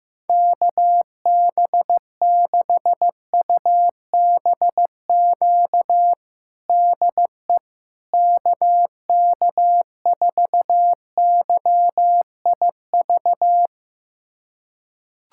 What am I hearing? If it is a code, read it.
Morse code, heard as a keyed tone, about 16 wpm, 700 Hz. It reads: KB6UBQ DE KK4YIV
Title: USA Callsign DE Callsign - 6 Characters - 15wpm